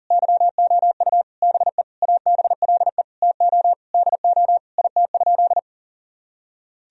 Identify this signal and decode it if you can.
Morse code, heard as a keyed tone, about 37 wpm, 700 Hz. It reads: YOU BE ABLE TO DO IT?
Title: Sentences from Top 1000 Words - Review Part 2 - 40wpm